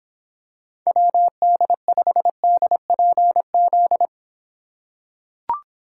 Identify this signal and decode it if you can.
Morse code, heard as a keyed tone, about 26 wpm, 700 Hz. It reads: WD5DPZ